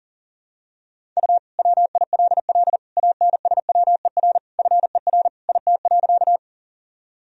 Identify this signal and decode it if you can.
Morse code, heard as a keyed tone, about 37 wpm, 700 Hz. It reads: U WILL ANSWER FER IT.
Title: Sentences from Top 300 Words and Abbreviations 40wpm